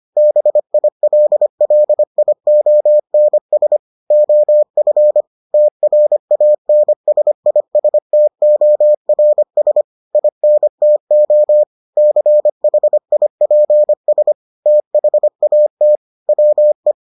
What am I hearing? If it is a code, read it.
Morse code, heard as a keyed tone, about 25 wpm, 600 Hz. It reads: BILLIONS OF TRANSISTORS INTO CHIPS THAT WE